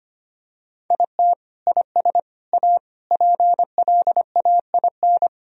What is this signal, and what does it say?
Morse code, heard as a keyed tone, about 25 wpm, 700 Hz. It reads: IT IS A PLAIN
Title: Sentences from Top 1000 Words - Review Part 2 - 25wpm